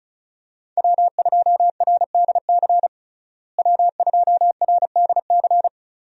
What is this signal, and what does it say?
Morse code, heard as a keyed tone, about 34 wpm, 700 Hz. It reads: W2RDC W2RDC
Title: USA Call Signs - 5 Characters - Repeated 3x - 35wpm